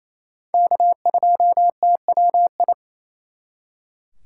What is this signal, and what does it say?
Morse code, heard as a keyed tone, about 34 wpm, 700 Hz. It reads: K2TWI
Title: Speed Racing - USA 1x3 Call Signs 28wpm